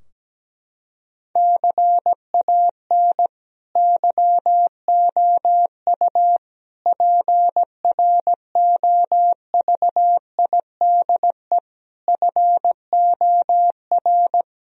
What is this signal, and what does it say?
Morse code, heard as a keyed tone, about 17 wpm, 700 Hz. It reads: CAN YOU PROVIDE FOR
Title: Sentences from Top 1000 Words - Review Part 2 - 17wpm